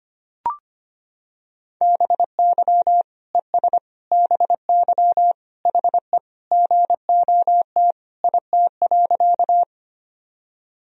Morse code, heard as a keyed tone, about 25 wpm, 700 Hz. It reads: BY ES BY HE GOT IT.
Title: Sentences from Top 300 Words and Abbreviations 25wpm